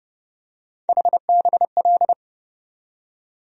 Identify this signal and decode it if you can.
Morse code, heard as a keyed tone, about 31 wpm, 700 Hz. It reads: HBL
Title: USA Call Sign Postfixes - 3 Letters 30wpm